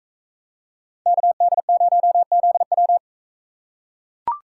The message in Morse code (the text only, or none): KD0ZW